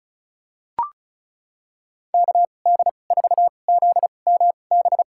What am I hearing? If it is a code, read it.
Morse code, heard as a keyed tone, about 34 wpm, spaced as for 25 wpm, 700 Hz. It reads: KD4ZMB
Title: USA Call Signs - 6 Characters - 2x Character Spacing - 35wpm